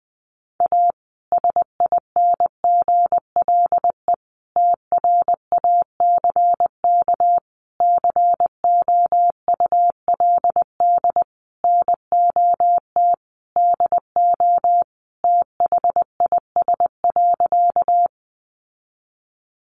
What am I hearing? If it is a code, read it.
Morse code, heard as a keyed tone, about 20 wpm, 700 Hz. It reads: A SINGLE TRACK COULD NOT DO THIS.
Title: Sentences from Top 1100 Words 20wpm